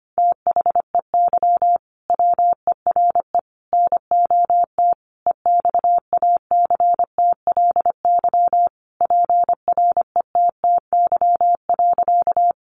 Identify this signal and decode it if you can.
Morse code, heard as a keyed tone, about 25 wpm, 700 Hz. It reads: THEY WERE NOT EXACTLY PRETTY.
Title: Sentences from Top 1000 Words 25wpm